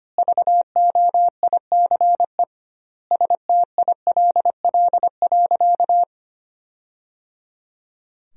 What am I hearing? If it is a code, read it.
Morse code, heard as a keyed tone, about 25 wpm, 700 Hz. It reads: VOICE STILL.
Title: 2x Word-Spacing - Sentences from Top 500 Words - 25wpm